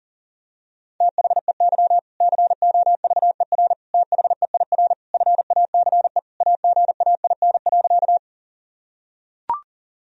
Morse code, heard as a keyed tone, about 40 wpm, 700 Hz. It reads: THEY COVER THEIR FACE AGAIN.
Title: Sentences from Top 1000 Words - Review Part 1 - 40wpm